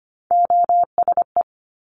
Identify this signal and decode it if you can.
Morse code, heard as a keyed tone, about 25 wpm, 700 Hz. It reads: OSE
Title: Top Three Letter Combinations in Words 25wpm